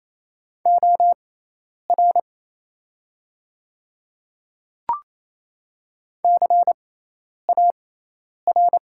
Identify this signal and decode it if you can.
Morse code, heard as a keyed tone, about 28 wpm, spaced as for 10 wpm, 700 Hz. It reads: OR CAR
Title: 6x Character Spacing - Top 300 Words - 28wpm